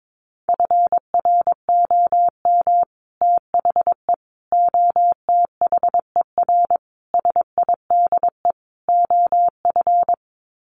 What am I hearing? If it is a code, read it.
Morse code, heard as a keyed tone, about 22 wpm, 700 Hz. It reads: FROM THE OTHER SIDE OF
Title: Sets of 5 Words Encore 22wpm